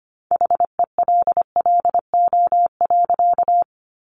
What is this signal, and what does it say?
Morse code, heard as a keyed tone, about 25 wpm, 700 Hz. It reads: HELLO.
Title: Sentences from Top 1100 Words 25wpm